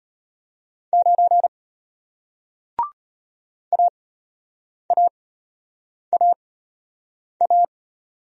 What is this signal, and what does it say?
Morse code, heard as a keyed tone, about 34 wpm, spaced as for 8 wpm, 700 Hz. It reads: 9 AAAA